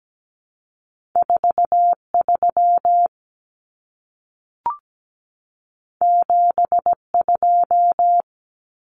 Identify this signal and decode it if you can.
Morse code, heard as a keyed tone, about 17 wpm, 700 Hz. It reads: 43 72
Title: Common Cut Numbers - 2-Digit 17wpm